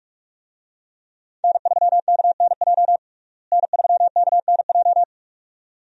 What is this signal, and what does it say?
Morse code, heard as a keyed tone, about 46 wpm, 700 Hz. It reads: N3KNJ N3KNJ
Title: USA Call Signs - 5 Characters - Repeated 2x - 45wpm